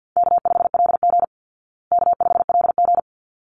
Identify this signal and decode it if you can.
Morse code, heard as a keyed tone, about 50 wpm, 700 Hz. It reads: K5LG K5LG
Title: Easy Call Signs Repeated 2x - 50wpm